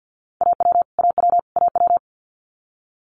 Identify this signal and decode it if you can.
Morse code, heard as a keyed tone, about 50 wpm, 690 Hz. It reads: AW AW AW
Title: Multi-letter Phonemes Repeated 3x 50wpm